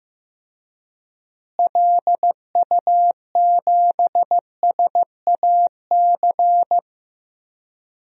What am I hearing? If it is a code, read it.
Morse code, heard as a keyed tone, about 15 wpm, 700 Hz. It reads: LU7SAC
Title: DXCC-100 Call Signs - 6 Characters - Repeated 3x - 15wpm